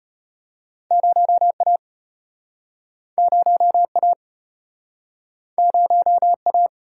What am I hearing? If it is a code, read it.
Morse code, heard as a keyed tone, about 34 wpm, 700 Hz. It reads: 0A 0A 0A